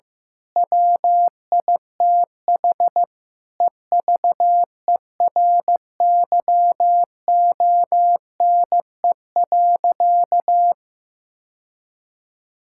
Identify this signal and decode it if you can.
Morse code, heard as a keyed tone, about 15 wpm, 700 Hz. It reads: WITH EVERYONE.
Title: Sentences from Top 1000 Words - Review Part 5 - 15wpm